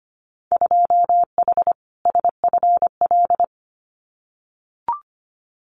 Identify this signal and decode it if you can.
Morse code, heard as a keyed tone, about 25 wpm, 700 Hz. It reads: 2H SFL
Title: Winter Field Day Exchanges Encore 25wpm